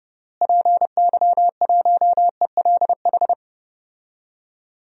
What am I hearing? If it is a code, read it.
Morse code, heard as a keyed tone, about 32 wpm, 700 Hz. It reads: PY1ELH
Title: DXCC-100 Call Signs - 6 Characters - Repeated 3x - 30wpm